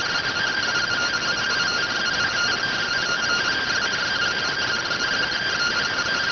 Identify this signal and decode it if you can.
Two-tone teleprinter signal, about 170 Hz shift.
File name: findings / 100 surview 0.gen-13 rx